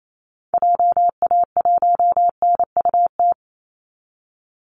Morse code, heard as a keyed tone, about 28 wpm, 700 Hz. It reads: JA1NUT
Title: Call Signs Repeated 2x - 28wpm